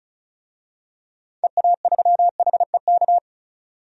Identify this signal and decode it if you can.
Morse code, heard as a keyed tone, about 34 wpm, 700 Hz. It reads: EA3HEK
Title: DXCC-100 Call Signs - 6 Characters - 35wpm